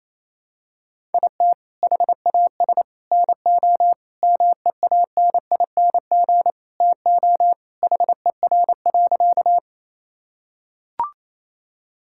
Morse code, heard as a keyed tone, about 28 wpm, 700 Hz. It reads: IT HAS NO MEANING TO HER.
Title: Sentences from Top 1000 Words - Review Part 6 - 28wpm